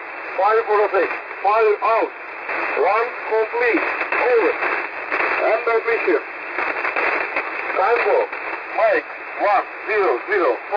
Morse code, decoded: none